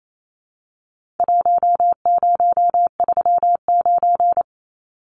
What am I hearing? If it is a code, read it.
Morse code, heard as a keyed tone, about 28 wpm, 700 Hz. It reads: 1039
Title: Numbers - Contest Style - 4-Digit 28wpm